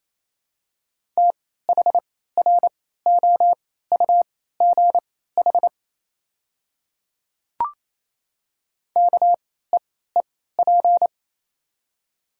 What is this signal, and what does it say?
Morse code, heard as a keyed tone, about 28 wpm, spaced as for 16 wpm, 700 Hz. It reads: THROUGH KEEP
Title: 3x Character Spacing - Top 300 Words - 28wpm